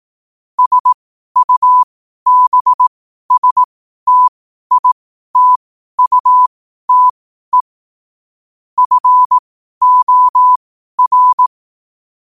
Morse code, heard as a keyed tone, about 18 wpm, spaced as for 12 wpm, 1000 Hz. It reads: SUBSTITUTE FOR
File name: qotdfiles / 12wpm